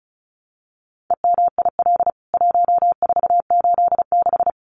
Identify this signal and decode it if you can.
Morse code, heard as a keyed tone, about 34 wpm, 700 Hz. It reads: EMIL 1486
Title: CWT Exchange 35wpm